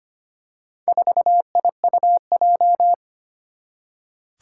Morse code, heard as a keyed tone, about 25 wpm, 700 Hz. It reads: 4IUJ
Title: Four Challenging Letter-Number Combinations 25wpm